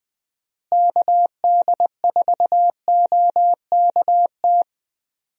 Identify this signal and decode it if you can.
Morse code, heard as a keyed tone, about 20 wpm, 700 Hz. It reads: KD4OKT
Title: USA Call Signs - 6 Characters - Repeated 3x - 20wpm